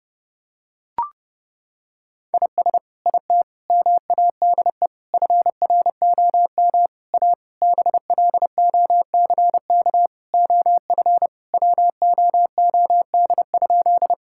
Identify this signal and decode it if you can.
Morse code, heard as a keyed tone, about 30 wpm, 700 Hz. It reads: IS IT MADE FROM A BLOCK OF WOOD?
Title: Sentences from Top 1600 Words 30wpm